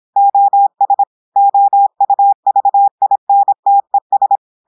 Morse code, heard as a keyed tone, about 26 wpm, 800 Hz. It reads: OS OUVINTES
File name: portugais texte en clair 2 vitesse 26 mots minutes